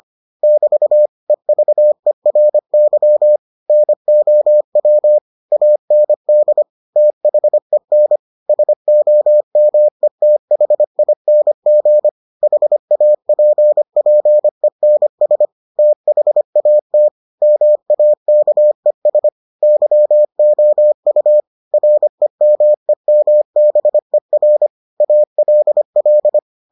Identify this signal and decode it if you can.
Morse code, heard as a keyed tone, about 25 wpm, 600 Hz. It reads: <BT> EVERY NOW AND THEN SOMETHING HAPPENS THAT MAKES YOU REMEMBER ALL